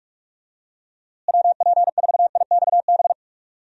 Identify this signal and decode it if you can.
Morse code, heard as a keyed tone, about 41 wpm, 700 Hz. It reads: WP4IXB
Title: USA Call Signs - 6 Characters - Repeated 2x - 45wpm